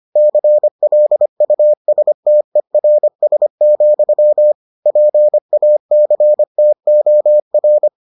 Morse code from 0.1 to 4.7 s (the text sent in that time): CLUSTERS,